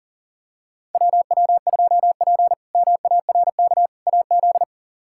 Morse code, heard as a keyed tone, about 40 wpm, 700 Hz. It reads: WW2P MARK AZ
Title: K1USN SST QSO Practice - 40wpm